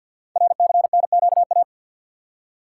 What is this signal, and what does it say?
Morse code, heard as a keyed tone, about 48 wpm, 700 Hz. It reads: RCNQA
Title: Code Groups - 5 Letters - 50wpm